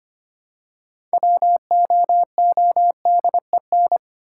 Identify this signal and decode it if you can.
Morse code, heard as a keyed tone, about 25 wpm, 700 Hz. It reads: WOODEN
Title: Six Letter Words - 25wpm